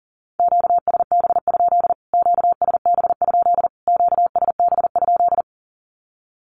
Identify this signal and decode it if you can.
Morse code, heard as a keyed tone, about 40 wpm, 700 Hz. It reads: QSB? QSB? QSB?
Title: Q-Codes Repeated 3x 40wpm